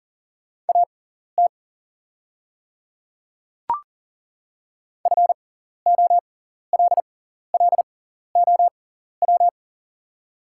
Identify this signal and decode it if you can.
Morse code, heard as a keyed tone, about 40 wpm, spaced as for 14 wpm, 690 Hz. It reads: AT FOLLOW
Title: 6x Character Spacing - Top 200 Words - 40wpm